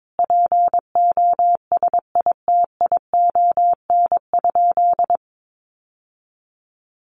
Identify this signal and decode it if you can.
Morse code, heard as a keyed tone, about 22 wpm, 700 Hz. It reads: POSITION?
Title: Sentences from Top 900 Words 22wpm